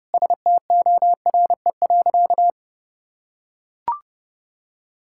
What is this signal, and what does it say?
Morse code, heard as a keyed tone, about 30 wpm, 700 Hz. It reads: STORE.
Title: Sentences from Top 1000 Words - Review Part 5 - 30wpm